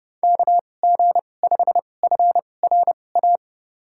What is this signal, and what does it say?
Morse code, heard as a keyed tone, about 30 wpm, spaced as for 22 wpm, 700 Hz. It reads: KG5FRA
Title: USA Call Signs - 6 Characters - 2x Character Spacing - 30wpm